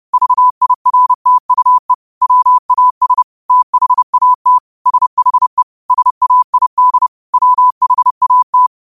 Morse code, heard as a keyed tone, about 30 wpm, 1000 Hz. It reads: VIRTUE WAS THAT SHE SAID WHAT
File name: qotdfiles / 30wpm